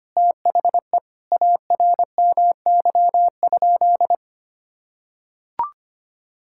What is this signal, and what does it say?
Morse code, heard as a keyed tone, about 25 wpm, 700 Hz. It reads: THE ARMY?
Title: Sentences from Top 1000 Words - Review Part 6 - 25wpm